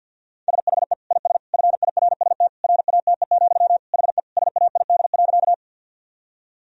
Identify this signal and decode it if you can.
Morse code, heard as a keyed tone, about 48 wpm, 700 Hz. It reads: SHE IS FIRST RATE, HE SAID.
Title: Sentences from Top 1000 Words - Review Part 4 - 50wpm